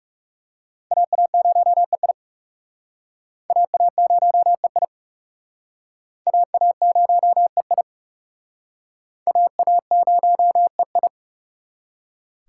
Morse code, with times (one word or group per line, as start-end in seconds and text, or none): 0.9-2.1 AA0EI
3.5-4.8 AA0EI
6.3-7.8 AA0EI
9.3-11.1 AA0EI